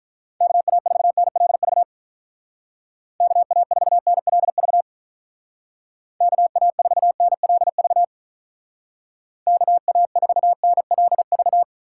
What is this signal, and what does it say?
Morse code, heard as a keyed tone, about 43 wpm, 700 Hz. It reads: KA4NLV KA4NLV KA4NLV KA4NLV
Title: Speed Racing - USA 2x3 Call Signs 35wpm